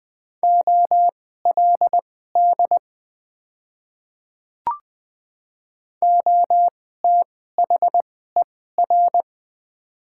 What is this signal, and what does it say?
Morse code, heard as a keyed tone, about 20 wpm, spaced as for 15 wpm, 700 Hz. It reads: OLD OTHER
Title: 2x Character Spacing - Top 200 Words - 20wpm